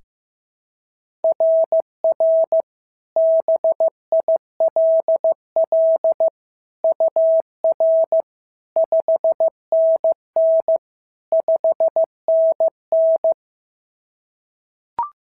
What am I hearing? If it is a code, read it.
Morse code, heard as a keyed tone, about 14 wpm, 650 Hz. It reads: RR BILL UR 5NN 5NN
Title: SOTA QSO Practice 15wpm